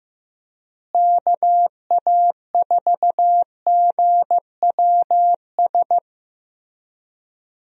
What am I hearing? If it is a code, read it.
Morse code, heard as a keyed tone, about 15 wpm, 700 Hz. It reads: KA4GWS
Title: USA Call Signs - 6 Characters - Repeated 2x - 15wpm